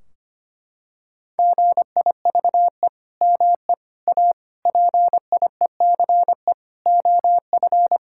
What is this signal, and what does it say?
Morse code, heard as a keyed tone, about 25 wpm, 700 Hz. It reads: GIVE ME A PIECE OF
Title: Sentences from Top 1600 Words 25wpm